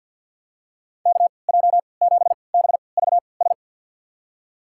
Morse code, heard as a keyed tone, about 52 wpm, spaced as for 28 wpm, 700 Hz. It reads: KJ7BVS
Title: USA Call Signs - 3x Character Spacing - 50wpm